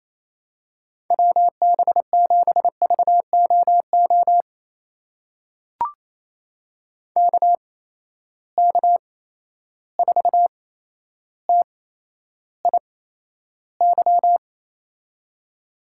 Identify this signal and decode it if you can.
Morse code, heard as a keyed tone, about 28 wpm, spaced as for 12 wpm, 700 Hz. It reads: WB7VOO KK4TIY
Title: USA Call Signs - 6 Characters - 8x Character Spacing - 28wpm